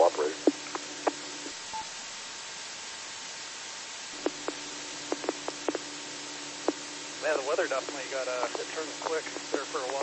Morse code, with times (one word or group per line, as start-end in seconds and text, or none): none